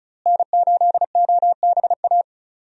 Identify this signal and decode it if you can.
Morse code, heard as a keyed tone, about 34 wpm, 700 Hz. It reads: N8OBA
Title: USA Call Signs - 5 Characters - Repeated 3x - 35wpm